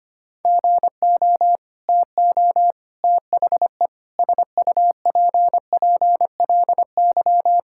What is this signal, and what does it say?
Morse code, heard as a keyed tone, about 25 wpm, 700 Hz. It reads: GO TO THE SUPPLY
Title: Sentences from Top 1000 Words - Review Part 5 - 25wpm